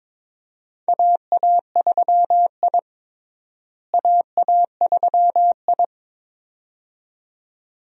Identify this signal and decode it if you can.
Morse code, heard as a keyed tone, about 22 wpm, 700 Hz. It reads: AA3I AA3I
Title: USA Call Signs - 4 Characters - Repeated 2x - 22wpm